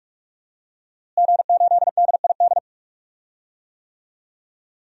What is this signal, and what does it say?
Morse code, heard as a keyed tone, about 42 wpm, 700 Hz. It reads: G8DID
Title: DXCC-100 Call Signs - 5 Characters - Repeated 3x - 45wpm